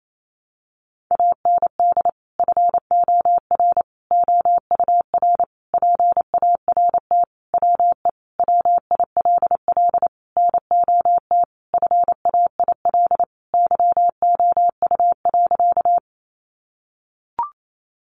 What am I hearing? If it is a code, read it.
Morse code, heard as a keyed tone, about 28 wpm, 700 Hz. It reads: AND FOR OUR PART WE WILL NOT FAIL YOU.
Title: Sentences from Top 1100 Words 28wpm